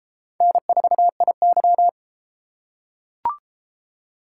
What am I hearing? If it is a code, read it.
Morse code, heard as a keyed tone, about 32 wpm, 700 Hz. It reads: N4IY